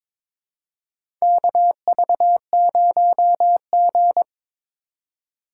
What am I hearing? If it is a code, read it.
Morse code, heard as a keyed tone, about 22 wpm, 700 Hz. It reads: KV0G
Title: USA Call Signs - 4 Characters - 2x Character Spacing - 22wpm